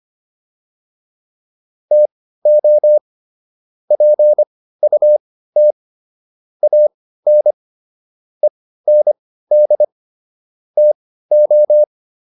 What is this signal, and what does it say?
Morse code, heard as a keyed tone, about 25 wpm, spaced as for 15 wpm, 600 Hz. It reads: TO PUT AN END TO
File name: WD404 15